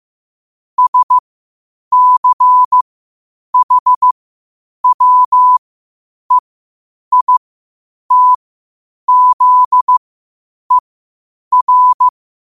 Morse code, heard as a keyed tone, about 15 wpm, spaced as for 8 wpm, 1000 Hz. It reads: SCHWEITZER
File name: qotdfiles / 8wpm